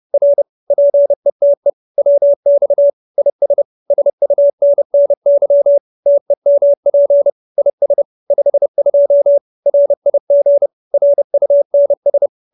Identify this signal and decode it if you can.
Morse code, heard as a keyed tone, about 30 wpm, 570 Hz. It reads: R PETE WX IS SUNNY TEMP IS 52 RIG RUNS
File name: QSO308 30